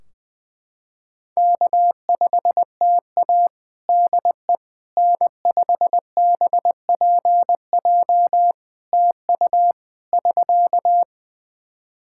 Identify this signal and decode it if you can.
Morse code, heard as a keyed tone, about 21 wpm, 700 Hz. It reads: K5TA DE N5BPJ TU <SK>
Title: K1USN SST QSO Practice 20wpm